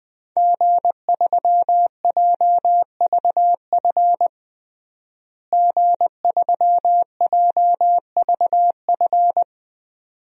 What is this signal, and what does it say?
Morse code, heard as a keyed tone, about 20 wpm, 700 Hz. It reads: G3JVF G3JVF
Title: DXCC-100 Call Signs - 5 Characters - Repeated 2x - 20wpm